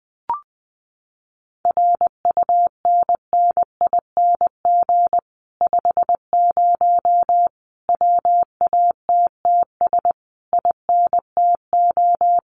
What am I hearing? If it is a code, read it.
Morse code, heard as a keyed tone, about 20 wpm, 700 Hz. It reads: RUNNING 50 WATTS INTO
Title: Common QSO Phrases Encore 20wpm